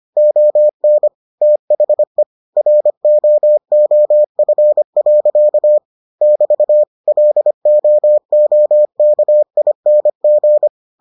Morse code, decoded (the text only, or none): ON THE ROOF. <BT> LOOKING